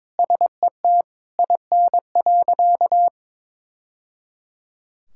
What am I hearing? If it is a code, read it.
Morse code, heard as a keyed tone, about 22 wpm, 700 Hz. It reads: SET IN.
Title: Sentences from Top 1000 Words - Review Part 5 - 22wpm